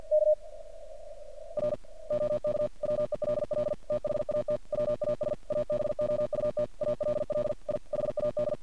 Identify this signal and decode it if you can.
Morse code, heard as a keyed tone, about 50 wpm, 600 Hz. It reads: K R OK WELL THAT WAS ABOUT ALL I HAD
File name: archived / CRSnet